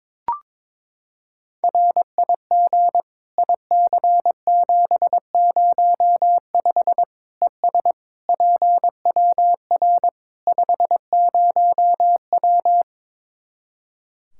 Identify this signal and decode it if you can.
Morse code, heard as a keyed tone, about 21 wpm, 700 Hz. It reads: RIG IC705 ES PWR 50W
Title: Practice QSO - Exchange 2 - Info - 22wpm